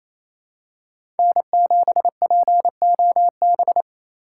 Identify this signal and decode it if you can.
Morse code, heard as a keyed tone, about 28 wpm, 700 Hz. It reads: N7POB
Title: USA Call Signs - 6x Character Spacing - 28wpm